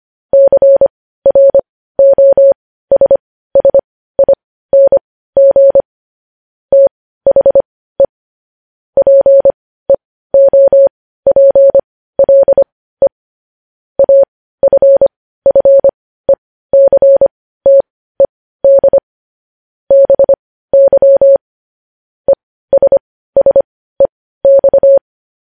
Morse code, heard as a keyed tone, about 25 wpm, spaced as for 15 wpm, 550 Hz. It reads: CROSSING THE PEOPLE AFFECTED BY ESSEX
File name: podcast / News headlines-1x-25x15